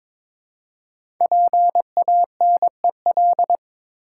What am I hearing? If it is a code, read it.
Morse code, heard as a keyed tone, about 22 wpm, 700 Hz. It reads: PANEL
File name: 5 Letter Words - First 3 Letters Given - 22wpm